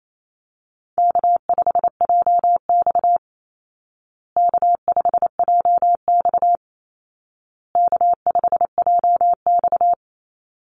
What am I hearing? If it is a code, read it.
Morse code, heard as a keyed tone, about 28 wpm, 700 Hz. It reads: K5JX K5JX K5JX
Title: Call Signs Repeated 2x - 28wpm